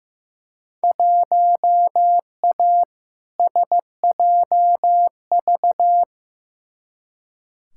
Morse code, heard as a keyed tone, about 15 wpm, 700 Hz. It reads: 1A SJV
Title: Field Day Exchanges Encore - 15wpm